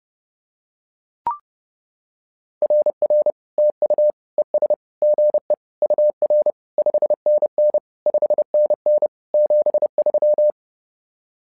Morse code, heard as a keyed tone, about 31 wpm, 600 Hz. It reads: RR TU ES GE UR 5NN 5NN 73
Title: SOTA QSO Practice 30wpm